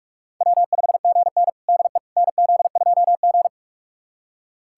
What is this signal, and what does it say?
Morse code, heard as a keyed tone, about 46 wpm, 700 Hz. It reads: W5GN DE NZ2G
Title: USA Callsign DE Callsign - 4 Characters - 45wpm